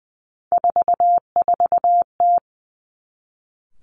Morse code, heard as a keyed tone, about 20 wpm, 700 Hz. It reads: 44T
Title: Three Challenging Letter-Number Combinations 20wpm